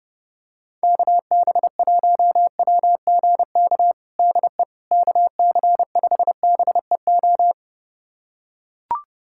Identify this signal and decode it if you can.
Morse code, heard as a keyed tone, about 30 wpm, 700 Hz. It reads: KB1WGK DE KC5BEO
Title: USA Callsign DE Callsign - 6 Characters - 30wpm